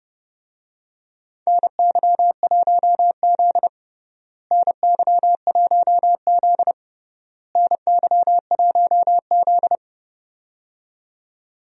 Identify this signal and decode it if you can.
Morse code, heard as a keyed tone, about 32 wpm, 700 Hz. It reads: NY1Z NY1Z NY1Z
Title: USA Call Signs - 4 Characters - Repeated 3x - 30wpm